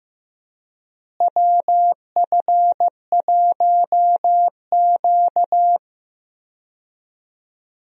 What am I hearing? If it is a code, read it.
Morse code, heard as a keyed tone, about 15 wpm, 700 Hz. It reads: WF1Q
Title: USA Call Signs - 4 Characters - Repeated 3x - 15wpm